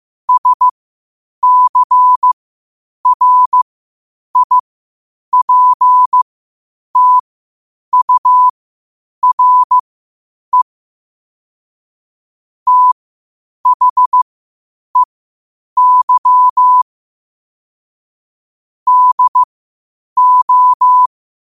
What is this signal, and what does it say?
Morse code, heard as a keyed tone, about 15 wpm, spaced as for 8 wpm, 1000 Hz. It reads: SCRIPTURE THEY DO
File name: qotdfiles / 8wpm